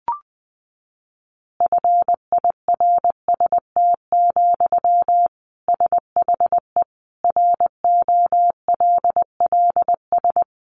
Morse code, heard as a keyed tone, about 20 wpm, 700 Hz. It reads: FIRST, SHE ROLLS
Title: Easy Phrases 20wpm